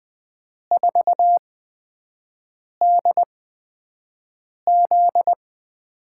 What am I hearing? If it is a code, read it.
Morse code, heard as a keyed tone, about 20 wpm, spaced as for 5 wpm, 700 Hz. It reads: 4DZ
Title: USA Call Signs - 8x Character Spacing - 20wpm